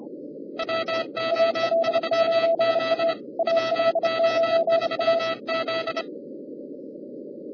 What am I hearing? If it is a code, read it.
Morse code, heard as a keyed tone, about 25 wpm, 1950 Hz, with noise. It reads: WO3Z WO3Z